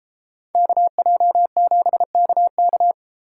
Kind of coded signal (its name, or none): Morse code